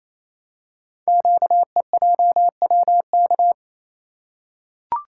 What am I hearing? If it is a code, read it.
Morse code, heard as a keyed tone, about 27 wpm, 700 Hz. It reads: QEJWK
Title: Code Groups - 5 Letters - 28wpm